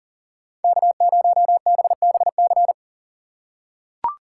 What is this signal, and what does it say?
Morse code, heard as a keyed tone, about 40 wpm, 700 Hz. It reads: K0BBC
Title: USA Call Signs - 5 Characters - 2x Character Spacing - 40wpm